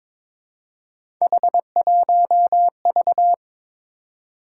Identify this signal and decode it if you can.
Morse code, heard as a keyed tone, about 23 wpm, 700 Hz. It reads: H1V
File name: practice / 22wpm / Three Letter-Number Combinations - 22wpm